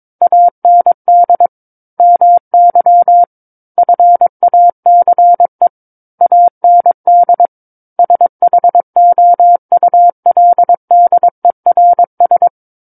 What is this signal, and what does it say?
Morse code, heard as a keyed tone, about 22 wpm, 700 Hz. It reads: AND MY FACE AND SHOULDERS
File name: Book One / 13-Chapter 13 - 22 WPM